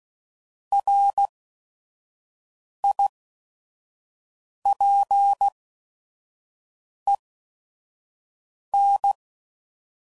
Morse code, heard as a keyed tone, about 16 wpm, spaced as for 5 wpm, 800 Hz. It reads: RIPEN
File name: Prefixes / By First Letter / Rs